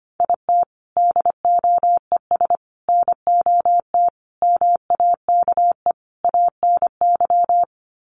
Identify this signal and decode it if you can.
Morse code, heard as a keyed tone, about 25 wpm, 700 Hz. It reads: IT DOES NOT MAKE ANY
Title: Sets of 5 Words Encore2 25wpm